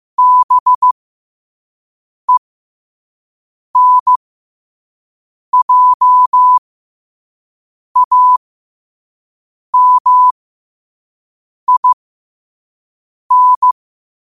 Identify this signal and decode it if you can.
Morse code, heard as a keyed tone, about 15 wpm, spaced as for 5 wpm, 1000 Hz. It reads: BENJAMIN
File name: qotdfiles / 5wpm